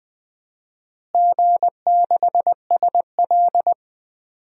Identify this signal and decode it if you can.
Morse code, heard as a keyed tone, about 20 wpm, 700 Hz. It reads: G6SL
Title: DXCC-100 Call Signs - 4 Characters - Repeated 2x - 20wpm